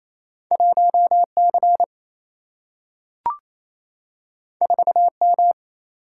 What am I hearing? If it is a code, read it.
Morse code, heard as a keyed tone, about 28 wpm, 700 Hz. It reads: 1C 4M
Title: Single-Letter Single-Number 28wpm